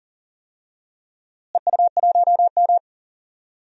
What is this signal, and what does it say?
Morse code, heard as a keyed tone, about 38 wpm, 700 Hz. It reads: EU1M
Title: DXCC-100 Call Signs - 4 Characters - Repeated 2x - 40wpm